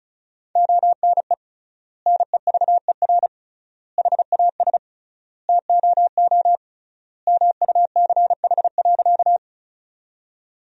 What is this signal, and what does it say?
Morse code, heard as a keyed tone, about 36 wpm, 700 Hz. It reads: ONE NEVER HAS TOO MUCH.
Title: 3x Word-Spacing - Sentences from Top 300 Words - 35wpm